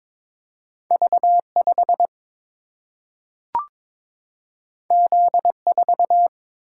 Morse code, heard as a keyed tone, about 22 wpm, 700 Hz. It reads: V 5 Z 4
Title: Single-Letter Single-Number 22wpm